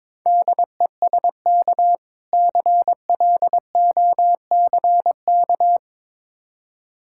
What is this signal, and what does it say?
Morse code, heard as a keyed tone, about 22 wpm, 690 Hz. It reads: DESK CLOCK
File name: Easy Words - Speed-Racing with Hints - 22wpm